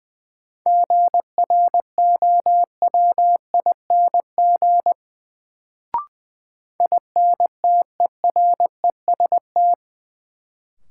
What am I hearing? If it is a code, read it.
Morse code, heard as a keyed tone, about 20 wpm, 700 Hz. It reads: GROWING INTEREST
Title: Sets of 5 Words Encore - New Word Drill - 20wpm